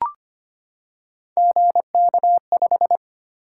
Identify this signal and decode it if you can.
Morse code, heard as a keyed tone, about 25 wpm, 700 Hz. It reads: GK5
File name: Three Letter-Number Combinations - 25wpm